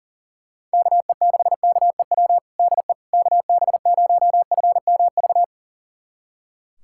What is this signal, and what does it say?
Morse code, heard as a keyed tone, about 40 wpm, 700 Hz. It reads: KE6KEW DE KB0FMV
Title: USA Callsign DE Callsign 40wpm